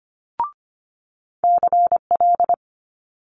Morse code, heard as a keyed tone, about 25 wpm, 700 Hz. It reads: CL
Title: Multi-letter Phonemes Quick Fire 25wpm